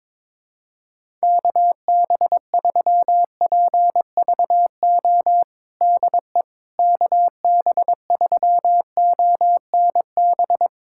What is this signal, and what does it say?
Morse code, heard as a keyed tone, about 22 wpm, 700 Hz. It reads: KB3PVO DE KB3ONB
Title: USA Callsign DE Callsign - 6 Characters - 22wpm